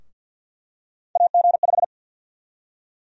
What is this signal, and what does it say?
Morse code, heard as a keyed tone, about 48 wpm, 700 Hz. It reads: AG5
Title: USA Call Sign Prefixes - 50wpm